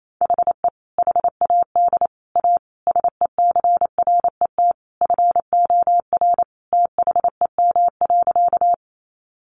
Morse code, heard as a keyed tone, about 28 wpm, 700 Hz. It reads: HE HAD A SECRET FOR THEM.
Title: Sentences from Top 1200 Words 28wpm